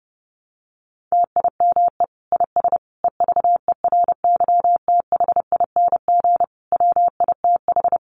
Morse code, heard as a keyed tone, about 30 wpm, 700 Hz. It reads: TIME IS EVERYTHING WITH
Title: Sentences from Top 700 Words 30wpm